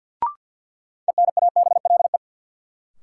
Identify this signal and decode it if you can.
Morse code, heard as a keyed tone, about 53 wpm, 700 Hz. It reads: ENABLE
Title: Sentences from Top 1500 Words - New Word Drill - 50wpm